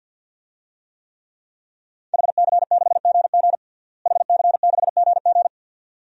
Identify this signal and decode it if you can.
Morse code, heard as a keyed tone, about 48 wpm, 700 Hz. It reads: HC6GG HC6GG
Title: DXCC-100 Call Signs - 5 Characters - Repeated 2x - 50wpm